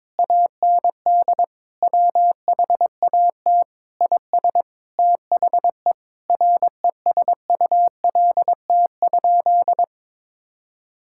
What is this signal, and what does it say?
Morse code, heard as a keyed tone, about 21 wpm, 700 Hz. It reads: AND WHAT IS THE RESULT?
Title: Sentences from Top 1600 Words 22wpm